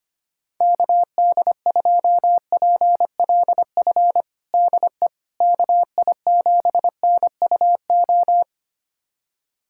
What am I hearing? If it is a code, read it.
Morse code, heard as a keyed tone, about 25 wpm, 700 Hz. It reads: KD2PLF DE KI7NUO
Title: USA Callsign DE Callsign - 6 Characters - 25wpm